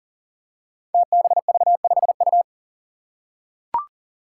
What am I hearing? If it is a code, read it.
Morse code, heard as a keyed tone, about 40 wpm, 700 Hz. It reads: TBV5U
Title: Five Challenging Letter-Number Combinations 40wpm